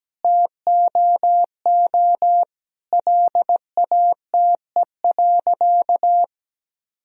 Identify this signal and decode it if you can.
Morse code, heard as a keyed tone, about 17 wpm, 700 Hz. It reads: TOO LATE.
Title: Sentences from Top 1000 Words - Review Part 4 - 17wpm